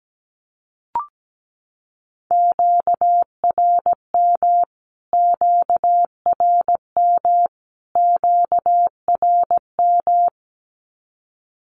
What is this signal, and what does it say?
Morse code, heard as a keyed tone, about 17 wpm, 700 Hz. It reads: QRM QRM QRM
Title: Q-Codes Repeated 3x 17wpm